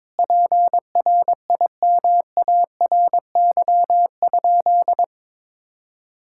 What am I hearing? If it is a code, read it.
Morse code, heard as a keyed tone, about 22 wpm, 700 Hz. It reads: PRIMARY?
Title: Sentences from Top 1000 Words - Review Part 5 - 22wpm